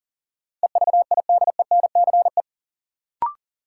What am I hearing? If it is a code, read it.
Morse code, heard as a keyed tone, about 40 wpm, 700 Hz. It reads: EVIDENCE
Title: Sentences from Top 1300 Words - New Word Drill - 40wpm